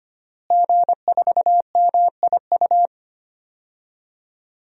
Morse code, heard as a keyed tone, about 25 wpm, 700 Hz. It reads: G4MIU
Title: DXCC-100 Call Signs - 5 Characters - Repeated 2x - 25wpm